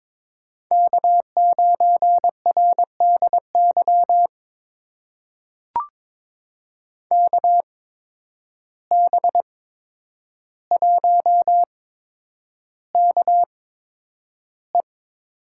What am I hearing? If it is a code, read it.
Morse code, heard as a keyed tone, about 22 wpm, spaced as for 10 wpm, 700 Hz. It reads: K9RDY KB1KE
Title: USA Call Signs - 5 Characters - 8x Character Spacing - 22wpm